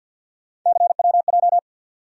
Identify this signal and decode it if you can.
Morse code, heard as a keyed tone, about 50 wpm, 700 Hz. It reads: CWJ
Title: USA Call Sign Postfixes - 3 Letters 50wpm